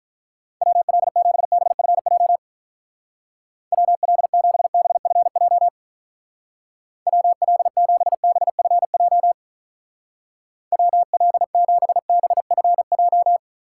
Morse code, heard as a keyed tone, about 44 wpm, 700 Hz. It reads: WL7BFJ WL7BFJ WL7BFJ WL7BFJ